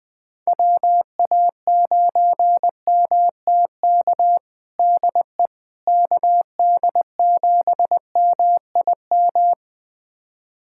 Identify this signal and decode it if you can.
Morse code, heard as a keyed tone, about 20 wpm, 700 Hz. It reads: WA9MTK DE KD7MIM
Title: USA Callsign DE Callsign - 6 Characters - 20wpm